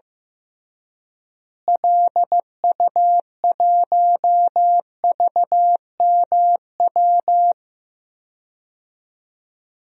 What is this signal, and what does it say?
Morse code, heard as a keyed tone, about 15 wpm, 700 Hz. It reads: LU1VMW
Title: DXCC-100 Call Signs - 6 Characters - Repeated 3x - 15wpm